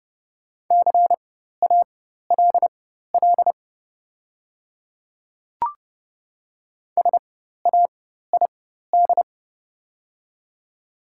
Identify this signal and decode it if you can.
Morse code, heard as a keyed tone, about 30 wpm, spaced as for 14 wpm, 700 Hz. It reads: CALL SAID
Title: 4x Character Spacing - Top 100 Words - 30wpm